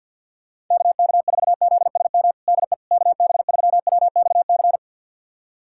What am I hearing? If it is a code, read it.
Morse code, heard as a keyed tone, about 49 wpm, 700 Hz. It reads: KK4ZSM DE KB3WXC
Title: USA Callsign DE Callsign - 6 Characters - 50wpm